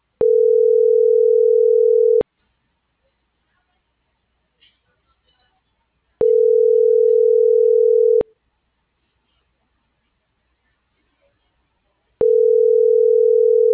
Ringback tone (North America), 440 + 480 Hz.